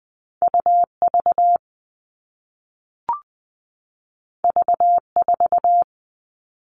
Morse code, heard as a keyed tone, about 20 wpm, 700 Hz. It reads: U V V 4
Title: All Cut Numbers - 2-Digit 20wpm